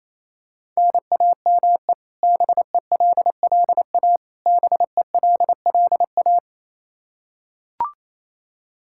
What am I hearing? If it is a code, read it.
Morse code, heard as a keyed tone, about 28 wpm, 700 Hz. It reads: NAME BELLA BELLA
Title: Practice QSO - Exchange 1 - Beginning - 28wpm